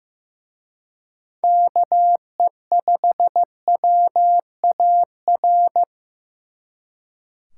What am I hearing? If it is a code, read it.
Morse code, heard as a keyed tone, about 18 wpm, 700 Hz. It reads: KE5WAR
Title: Speed Racing - USA 2x3 Call Signs 15wpm